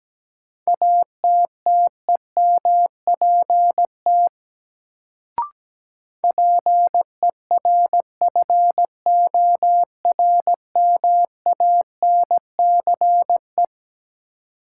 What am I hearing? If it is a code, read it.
Morse code, heard as a keyed tone, about 17 wpm, 700 Hz. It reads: ATTEMPT PERFORMANCE
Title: Sentences from Top 1000 Words - New Word Drill - 17wpm